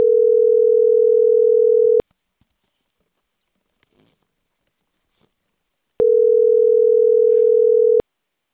Ringback tone (North America), 440 + 480 Hz.